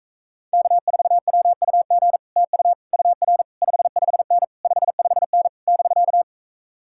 Morse code, heard as a keyed tone, about 46 wpm, 700 Hz. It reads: K4WUG TU UR 55N 55N <BK>